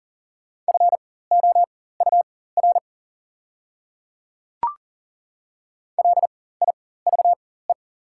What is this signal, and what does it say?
Morse code, heard as a keyed tone, about 40 wpm, spaced as for 19 wpm, 690 Hz. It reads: FOUR LIVE
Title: 4x Character Spacing - Top 300 Words - 40wpm